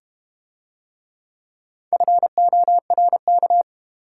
Morse code, heard as a keyed tone, about 30 wpm, 700 Hz. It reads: FORK
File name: Easy Words - Speed-Racing with Hints - 25wpm